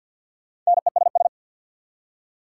Morse code, heard as a keyed tone, about 61 wpm, 690 Hz. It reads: NESS
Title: Speed-Racing Prefixes and Suffixes 50wpm